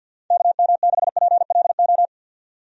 Morse code, heard as a keyed tone, about 51 wpm, 700 Hz. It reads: KM6PLO
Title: USA Call Signs - 6 Characters - 50wpm